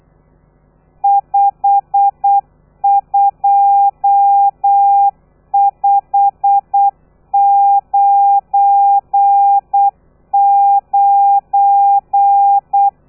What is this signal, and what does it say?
Morse code, heard as a keyed tone, about 8 wpm, 800 Hz. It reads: 52599
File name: telegrafia / i2tfj / Lez12 35